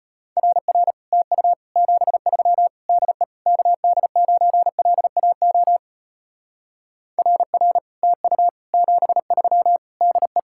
Morse code, heard as a keyed tone, about 33 wpm, 700 Hz. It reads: RR TU 73 DE KD9LAO RR TU 73 DE